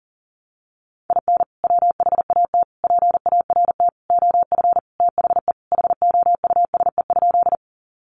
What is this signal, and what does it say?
Morse code, heard as a keyed tone, about 40 wpm, 700 Hz. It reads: IN WHAT PART OF THE HOUSE?
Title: Sentences from Top 200 Words 40wpm